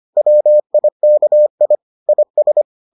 Morse code, heard as a keyed tone, about 25 wpm, 600 Hz. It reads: WIKI IS